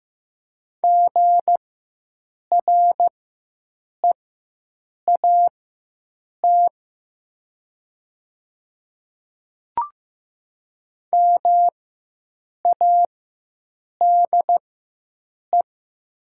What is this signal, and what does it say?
Morse code, heard as a keyed tone, about 15 wpm, spaced as for 7 wpm, 700 Hz. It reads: GREAT MADE
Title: 4x Character Spacing - Top 200 Words - 15wpm